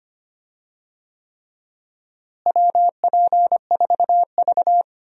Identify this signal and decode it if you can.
Morse code, heard as a keyed tone, about 25 wpm, 700 Hz. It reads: WP4V
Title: USA Call Signs - 4 Characters - Repeated 3x - 25wpm